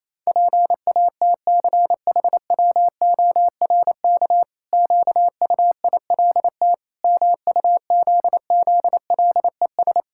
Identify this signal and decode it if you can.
Morse code, heard as a keyed tone, about 28 wpm, 700 Hz. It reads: PATCHWORK QUILT MUZZLES
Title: Pangram Challenge v2 - 28wpm